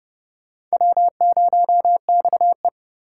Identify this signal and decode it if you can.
Morse code, heard as a keyed tone, about 37 wpm, 700 Hz. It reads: W0XE
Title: Speed Racing - USA 1x2 Call Signs 30wpm